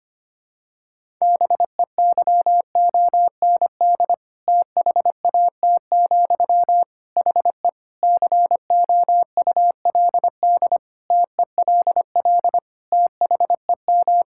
Morse code, heard as a keyed tone, about 25 wpm, 700 Hz. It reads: BEYOND THAT, HE COULD TELL THEM
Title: Sentences from Top 1000 Words - Review Part 6 - 25wpm